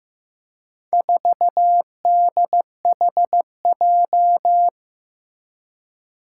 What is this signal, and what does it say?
Morse code, heard as a keyed tone, about 16 wpm, 700 Hz. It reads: 4DHJ
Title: Four Challenging Letter-Number Combinations 15wpm